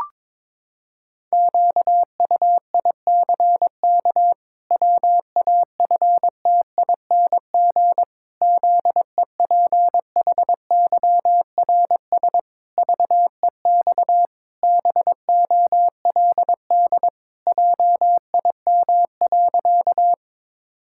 Morse code, heard as a keyed tone, about 22 wpm, 700 Hz. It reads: QUICK WAFTING ZEPHYRS VEX BOLD JIM.
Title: Pangram Challenge v2 - 22wpm